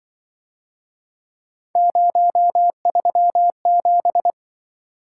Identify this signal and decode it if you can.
Morse code, heard as a keyed tone, about 25 wpm, 700 Hz. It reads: 037
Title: Speed-Racing - Three Letter-Number Combinations - 20wpm